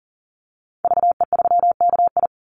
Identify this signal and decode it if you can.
Morse code, heard as a keyed tone, about 38 wpm, 700 Hz. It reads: VE3KI
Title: Call Signs Repeated 2x - 40wpm